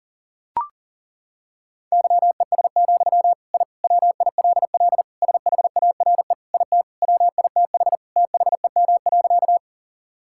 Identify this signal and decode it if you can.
Morse code, heard as a keyed tone, about 40 wpm, 700 Hz. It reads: YES, I WILL SHARE IT WITH THEM.
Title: Sentences from Top 1000 Words - Review Part 6 - 40wpm